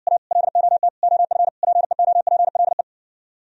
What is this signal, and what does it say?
Morse code, heard as a keyed tone, about 60 wpm, 700 Hz. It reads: A LOT OF PEOPLE